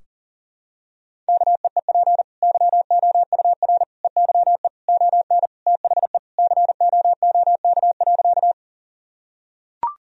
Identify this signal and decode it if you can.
Morse code, heard as a keyed tone, about 40 wpm, 700 Hz. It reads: KEEP YOUR EYE ON THE COOK.
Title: Sentences from Top 1000 Words - Review Part 4 - 40wpm